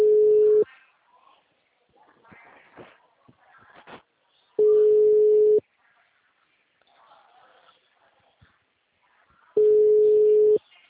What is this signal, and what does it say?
Ringback tone (Europe, 425 Hz).
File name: out- -2016- - - .23